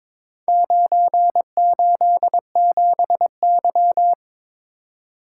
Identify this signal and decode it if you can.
Morse code, heard as a keyed tone, about 22 wpm, 690 Hz. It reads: 987Y
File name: practice / 22wpm / Four Letter-Number Combinations - 22wpm